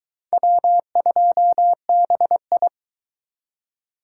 Morse code, heard as a keyed tone, about 23 wpm, 700 Hz. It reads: W2BI